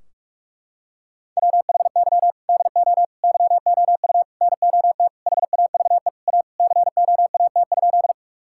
Morse code, heard as a keyed tone, about 46 wpm, 700 Hz. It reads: WHY DO YOU NOT HAVE A COAT?
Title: Sentences from Top 1600 Words 45wpm